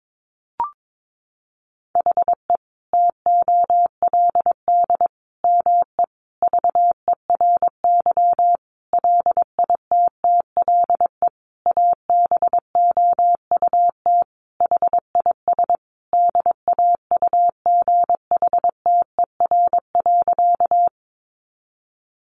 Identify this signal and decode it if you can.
Morse code, heard as a keyed tone, about 22 wpm, 700 Hz. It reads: HE TOLD ME VERY LITTLE ABOUT HIS DAUGHTER.
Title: Sentences from Top 1500 Words 22wpm